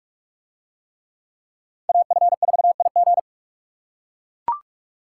Morse code, heard as a keyed tone, about 44 wpm, 700 Hz. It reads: AF4IG